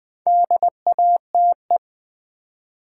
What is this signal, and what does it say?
Morse code, heard as a keyed tone, about 20 wpm, 700 Hz. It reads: DATE
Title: Build A Word - 2 Letters At A Time - 4-Letter Words - 20wpm